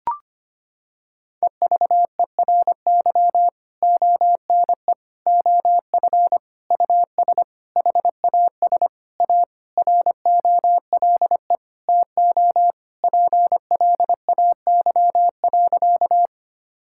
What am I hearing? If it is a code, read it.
Morse code, heard as a keyed tone, about 25 wpm, 700 Hz. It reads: EVERY ONE OF US HAS A ROLE TO PLAY.
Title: Sentences from Top 1000 Words - Review Part 6 - 25wpm